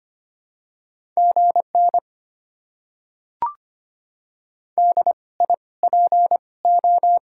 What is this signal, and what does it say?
Morse code, heard as a keyed tone, about 25 wpm, spaced as for 19 wpm, 700 Hz. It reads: GN DIPO
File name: Top 90 QSO Elements - 2x Character Spacing - 25wpm